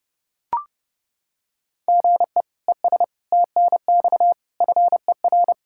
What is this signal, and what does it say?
Morse code, heard as a keyed tone, about 30 wpm, 700 Hz. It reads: GE ES TNX FER
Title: K1USN SST QSO Practice 30wpm